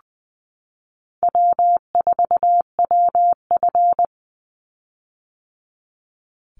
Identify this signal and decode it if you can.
Morse code, heard as a keyed tone, about 20 wpm, 700 Hz. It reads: W4WF
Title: Call Signs - 20wpm